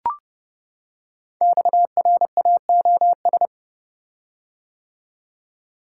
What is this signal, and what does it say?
Morse code, heard as a keyed tone, about 30 wpm, 700 Hz. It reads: XRAOS
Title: Code Groups - 5 Letters - 30wpm